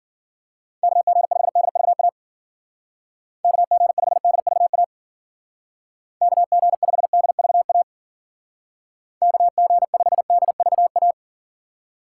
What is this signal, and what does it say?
Morse code, heard as a keyed tone, about 49 wpm, 700 Hz. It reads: KG5DVA KG5DVA KG5DVA KG5DVA
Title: Speed Racing - USA 2x3 Call Signs 40wpm